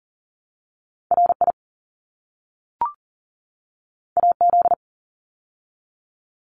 Morse code, heard as a keyed tone, about 40 wpm, 700 Hz. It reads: RI AZ
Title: ARRL Sections 40wpm